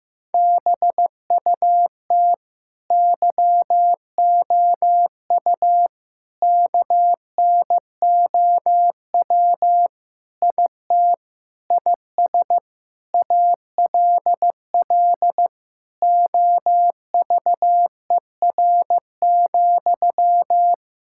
Morse code, heard as a keyed tone, about 15 wpm, 700 Hz. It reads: BUT YOU KNOW IT IS ALL OVER,
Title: Sentences from Top 1600 Words 15wpm